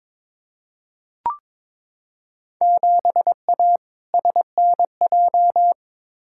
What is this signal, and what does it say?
Morse code, heard as a keyed tone, about 22 wpm, 700 Hz. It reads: 7A SNJ
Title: Field Day Exchanges Encore - 22wpm